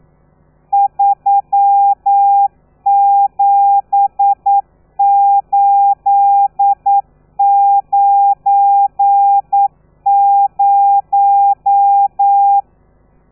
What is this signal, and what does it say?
Morse code, heard as a keyed tone, about 9 wpm, 800 Hz. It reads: 37890
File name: telegrafia / i2tfj / Lez12 40